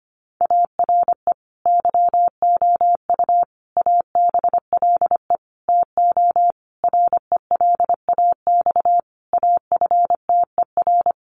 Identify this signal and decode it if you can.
Morse code, heard as a keyed tone, about 25 wpm, 700 Hz. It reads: ARE YOU ABLE TO RELAX AFTER
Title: Sentences from Top 1400 Words 25wpm